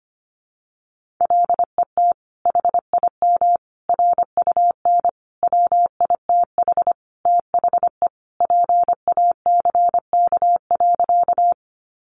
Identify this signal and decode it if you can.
Morse code, heard as a keyed tone, about 25 wpm, 700 Hz. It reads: LET HIM RUN WITH THE PACK.
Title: Sentences from Top 1300 Words 25wpm